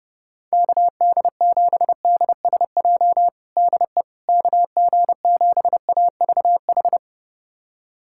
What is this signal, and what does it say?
Morse code, heard as a keyed tone, about 30 wpm, 700 Hz. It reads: KD7DSJ DE KG7AVH
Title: USA Callsign DE Callsign 30wpm